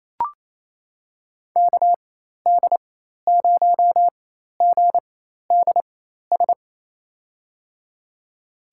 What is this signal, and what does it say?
Morse code, heard as a keyed tone, about 27 wpm, spaced as for 13 wpm, 700 Hz. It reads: KD0GDS
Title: USA Call Signs - 6 Characters - 4x Character Spacing - 28wpm